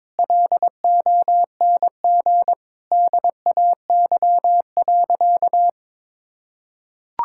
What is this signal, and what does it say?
Morse code, heard as a keyed tone, about 23 wpm, 700 Hz. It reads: LONG DAY.
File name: practice / 22wpm / Sentences with 2-Word Phrases - 22wpm